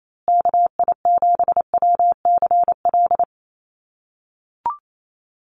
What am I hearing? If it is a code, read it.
Morse code, heard as a keyed tone, about 28 wpm, 700 Hz. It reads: KI7WCL
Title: USA Call Signs Repeated 3x 28wpm